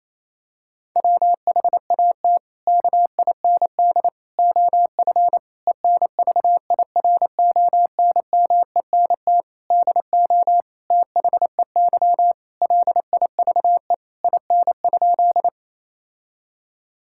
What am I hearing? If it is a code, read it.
Morse code, heard as a keyed tone, about 28 wpm, 700 Hz. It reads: WHAT KIND OF ENVIRONMENT DO THEY LIVE IN?
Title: Sentences from Top 1000 Words - Review Part 5 - 28wpm